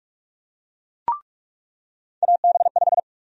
Morse code, heard as a keyed tone, about 45 wpm, 700 Hz. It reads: AB5
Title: USA Call Sign Prefixes - 45wpm